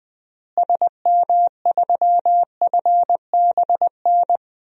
Morse code, heard as a keyed tone, about 20 wpm, 700 Hz. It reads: SM3FBN
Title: DXCC-100 Call Signs - 6 Characters - Repeated 2x - 20wpm